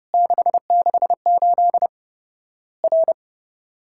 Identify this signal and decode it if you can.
Morse code, heard as a keyed tone, about 30 wpm, 700 Hz. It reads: 668 R
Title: Worked All Europe DX Contest - QTC Exchange - 30wpm